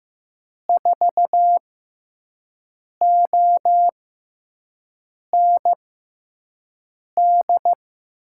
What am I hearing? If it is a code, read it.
Morse code, heard as a keyed tone, about 16 wpm, spaced as for 5 wpm, 700 Hz. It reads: 4OND
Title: USA Call Signs - 6 Characters - 6x Character Spacing - 15wpm